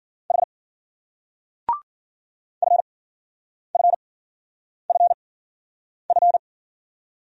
Morse code, heard as a keyed tone, about 50 wpm, 700 Hz. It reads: H F F F F
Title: Speed-Racing Single Letters 40wpm